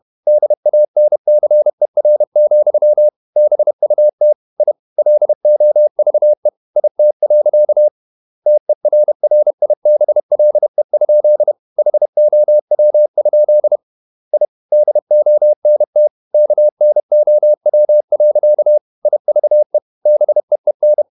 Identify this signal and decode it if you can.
Morse code, heard as a keyed tone, about 31 wpm, 600 Hz. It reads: DANCER, BUT I LOVE IT. TERRIBLE? HOW? I DONT KNOW. IVE BEEN
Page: Never 31wpm | CW med Gnister